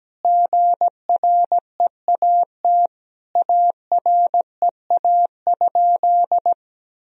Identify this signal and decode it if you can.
Morse code, heard as a keyed tone, about 17 wpm, 700 Hz. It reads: GREAT AREA?
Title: Sentences from Top 1000 Words - Review Part 2 - 17wpm